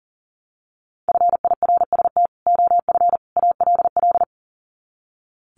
Morse code, heard as a keyed tone, about 40 wpm, 700 Hz. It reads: FIRST OF ALL
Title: Sets of 3 Words 40wpm